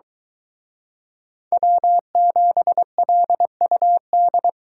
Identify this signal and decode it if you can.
Morse code, heard as a keyed tone, about 23 wpm, 700 Hz. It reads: W7LUD